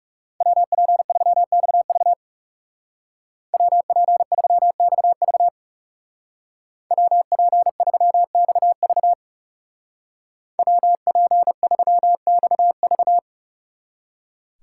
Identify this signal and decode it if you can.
Morse code, heard as a keyed tone, about 37 wpm, 700 Hz. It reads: WP3XV WP3XV WP3XV WP3XV
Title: Speed Racing - USA 2x2 Call Signs 30wpm